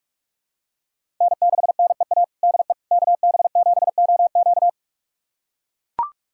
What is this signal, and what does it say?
Morse code, heard as a keyed tone, about 41 wpm, 700 Hz. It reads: N6NEA DE KB7OQ
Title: USA Callsign DE Callsign - 5 Characters - 45wpm